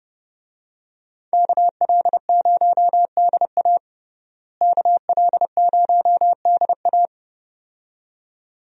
Morse code, heard as a keyed tone, about 30 wpm, 700 Hz. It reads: KL0DA KL0DA
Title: USA Call Signs - 5 Characters - Repeated 2x - 30wpm